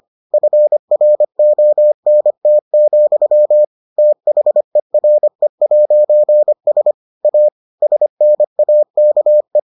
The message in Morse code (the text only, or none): FRONT, THERE'S A SNAKE